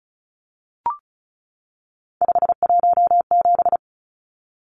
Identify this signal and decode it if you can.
Morse code, heard as a keyed tone, about 34 wpm, 700 Hz. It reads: 517
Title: Numbers - 3-Digit 35wpm